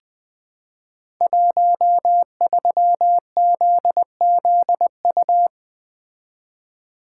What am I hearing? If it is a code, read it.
Morse code, heard as a keyed tone, about 20 wpm, 700 Hz. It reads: 13ZZU
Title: Code Groups - 5 Letter:Numbers - 20wpm